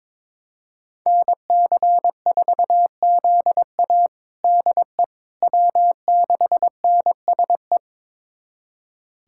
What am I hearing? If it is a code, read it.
Morse code, heard as a keyed tone, about 22 wpm, 700 Hz. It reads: NC4ZA DE W6NSE
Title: USA Callsign DE Callsign - 5 Characters - 22wpm